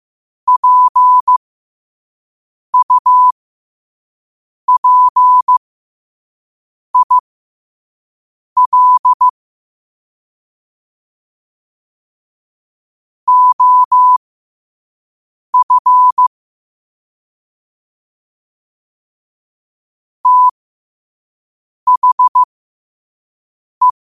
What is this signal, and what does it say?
Morse code, heard as a keyed tone, about 15 wpm, spaced as for 5 wpm, 1000 Hz. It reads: PUPIL OF THE
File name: qotdfiles / 5wpm